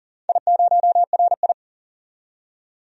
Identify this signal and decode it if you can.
Morse code, heard as a keyed tone, about 38 wpm, 700 Hz. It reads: I0RI
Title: DXCC-100 Call Signs - 4 Characters - 40wpm